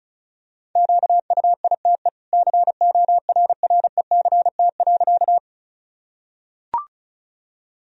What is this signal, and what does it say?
Morse code, heard as a keyed tone, about 34 wpm, 700 Hz. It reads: QUITE CORRECT.
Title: Sentences from Top 1000 Words - Review Part 2 - 35wpm